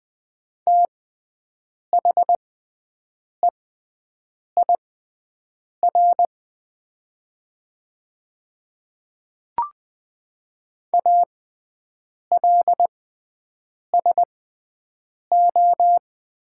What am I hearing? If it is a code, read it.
Morse code, heard as a keyed tone, about 20 wpm, spaced as for 7 wpm, 700 Hz. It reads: THEIR ALSO
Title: 6x Character Spacing - Top 200 Words - 20wpm